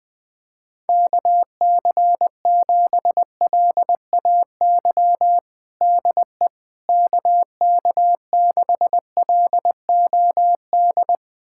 Morse code, heard as a keyed tone, about 20 wpm, 700 Hz. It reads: KC7LAY DE KK6LOD
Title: USA Callsign DE Callsign - 6 Characters - 20wpm